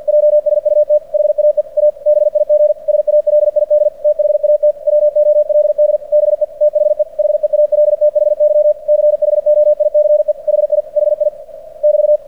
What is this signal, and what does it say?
Morse code, heard as a keyed tone, about 46 wpm, 600 Hz. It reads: JSUT HAE A BIG SALID THAT WOLD BE THE HEALTHY CHOICE HI HI <BT>
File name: archived / CRSnet